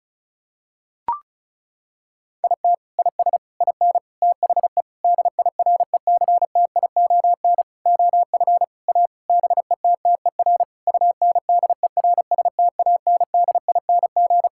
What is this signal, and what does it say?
Morse code, heard as a keyed tone, about 36 wpm, 700 Hz. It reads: IT IS IN THE DIRECTION OF A BETTER UNDERSTANDING
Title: Sentences from Top 1000 Words - Review Part 5 - 35wpm